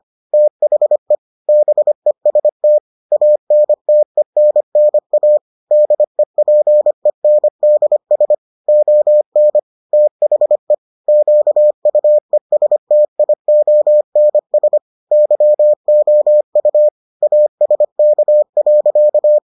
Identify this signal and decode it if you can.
Morse code, heard as a keyed tone, about 25 wpm, 600 Hz. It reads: THE BEST ANTENNA DEPENDS ON THE QUESTIONS YOU ASK.